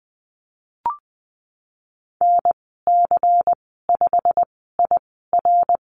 Morse code, heard as a keyed tone, about 20 wpm, spaced as for 15 wpm, 700 Hz. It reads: NC5IR
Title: 161 Lesson 26g - Letter g - Call Signs 20wpm